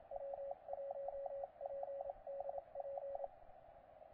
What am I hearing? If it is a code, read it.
Morse code, heard as a keyed tone, about 28 wpm, 560 Hz, with radio noise. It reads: W1PDP
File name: posts / 9l1jt-15cw